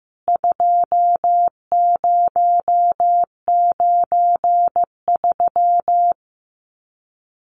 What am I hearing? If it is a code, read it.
Morse code, heard as a keyed tone, about 15 wpm, 700 Hz. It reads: 2093
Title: CWT Exchange 15wpm